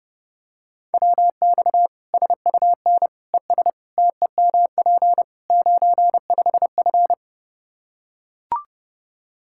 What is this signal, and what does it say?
Morse code, heard as a keyed tone, about 32 wpm, 700 Hz. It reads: WX SUN ES TEMP 95F
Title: K1USN SST QSO Practice 30wpm